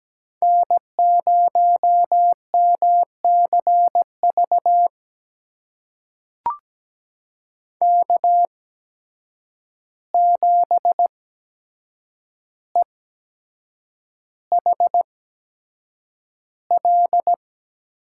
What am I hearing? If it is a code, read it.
Morse code, heard as a keyed tone, about 17 wpm, spaced as for 7 wpm, 700 Hz. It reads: N0MCV K7EHL
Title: USA Call Signs - 5 Characters - 8x Character Spacing - 17wpm